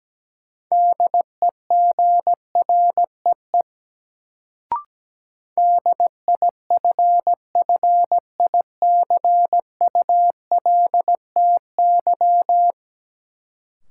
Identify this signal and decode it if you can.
Morse code, heard as a keyed tone, about 17 wpm, 700 Hz. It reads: DEGREE DIFFICULTY
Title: Sentences from Top 1500 Words - New Word Drill - 17wpm